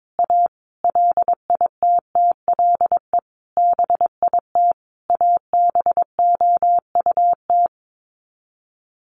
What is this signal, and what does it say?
Morse code, heard as a keyed tone, about 22 wpm, 700 Hz. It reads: A LITTLE BIT ABOUT
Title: Sets of 5 Words Encore 22wpm